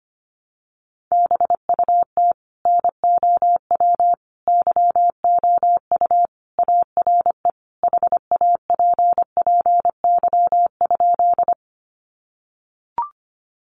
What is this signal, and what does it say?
Morse code, heard as a keyed tone, about 25 wpm, 700 Hz. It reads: BUT NOW YOU ARE HAPPY?
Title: Sentences from Top 1000 Words 25wpm